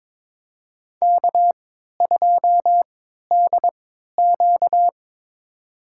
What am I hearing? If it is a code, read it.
Morse code, heard as a keyed tone, about 22 wpm, spaced as for 13 wpm, 700 Hz. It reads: K2DQ
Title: USA Call Signs - 4 Characters - 3x Character Spacing - 22wpm